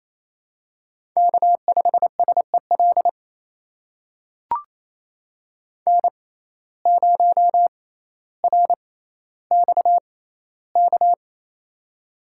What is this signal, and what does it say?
Morse code, heard as a keyed tone, about 28 wpm, spaced as for 14 wpm, 700 Hz. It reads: K5SEL N0RXK
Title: USA Call Signs - 6x Character Spacing - 28wpm